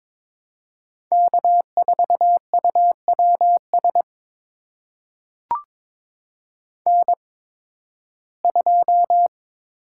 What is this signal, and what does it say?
Morse code, heard as a keyed tone, about 22 wpm, spaced as for 14 wpm, 700 Hz. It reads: K4UWS N2
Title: USA Call Signs - 5 Characters - 8x Character Spacing - 22wpm